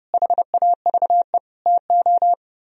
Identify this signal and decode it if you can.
Morse code, heard as a keyed tone, about 36 wpm, 700 Hz. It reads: HAVE TO
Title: Speed-Racing - Sets of 5 Words 30wpm